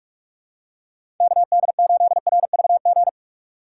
Morse code, heard as a keyed tone, about 46 wpm, 700 Hz. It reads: KD8RVG
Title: USA Call Signs - 6 Characters - 4x Character Spacing - 45wpm